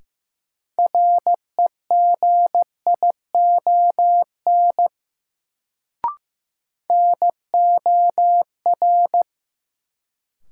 Morse code, heard as a keyed tone, about 15 wpm, 700 Hz. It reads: REGION NOR
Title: Sentences from Top 1500 Words - New Word Drill Alt - 15wpm